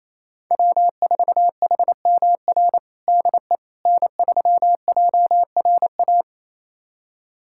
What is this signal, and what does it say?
Morse code, heard as a keyed tone, about 27 wpm, 700 Hz. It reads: W4HMR DE N3JRA
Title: USA Callsign DE Callsign - 5 Characters - 28wpm